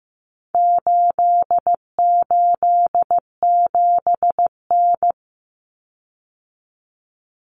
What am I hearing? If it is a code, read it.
Morse code, heard as a keyed tone, about 15 wpm, 700 Hz. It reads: 887N
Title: Common Cut Numbers - 4-Digit 15wpm